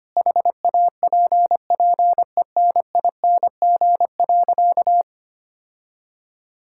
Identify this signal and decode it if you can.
Morse code, heard as a keyed tone, about 25 wpm, 700 Hz. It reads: HAPPENING.
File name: practice / 25wpm / All-Age Appropriate Jokes - 25wpm